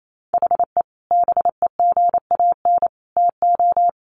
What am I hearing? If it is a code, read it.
Morse code, heard as a keyed tone, about 28 wpm, 700 Hz. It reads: HE BEGAN TO
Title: Sentences from Top 1500 Words 28wpm